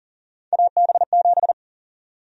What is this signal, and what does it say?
Morse code, heard as a keyed tone, about 40 wpm, 700 Hz. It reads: AB7
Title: USA Call Sign Prefixes - 40wpm